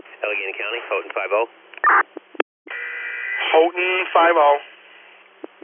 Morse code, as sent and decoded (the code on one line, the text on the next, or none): none